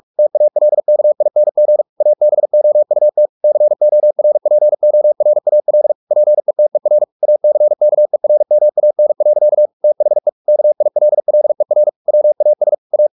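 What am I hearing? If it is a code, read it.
Morse code, heard as a keyed tone, about 44 wpm, 600 Hz. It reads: TALKING ABOUT CORPORAL PETER ACKERMAN. THE KILLER WAS A
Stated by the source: Morse practice at 45 wpm, the novel Never by Ken Follett in English, from CW med Gnister.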